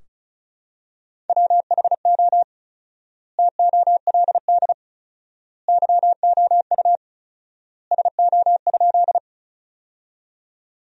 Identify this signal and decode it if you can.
Morse code, heard as a keyed tone, about 34 wpm, 700 Hz. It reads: WHO TOLD YOU SO?
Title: 4x Word-Spacing - Sentences from Top 400 Words - 35wpm